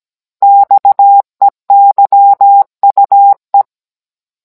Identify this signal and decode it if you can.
Morse code, heard as a keyed tone, about 17 wpm, 800 Hz. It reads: XEYUE
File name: Cours-NR19-16mm